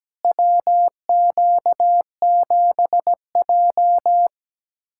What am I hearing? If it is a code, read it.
Morse code, heard as a keyed tone, about 17 wpm, 700 Hz. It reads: WQ7J
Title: USA Callsign DE Callsign - 4 Characters - 17wpm